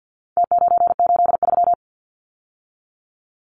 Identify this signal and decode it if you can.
Morse code, heard as a keyed tone, about 50 wpm, 700 Hz. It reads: T983
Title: Numbers - Contest Style - 4-Digit 50wpm